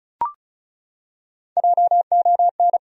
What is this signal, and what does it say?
Morse code, heard as a keyed tone, about 36 wpm, 700 Hz. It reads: JON
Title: USA Common Names - Max 5 Letters - 35wpm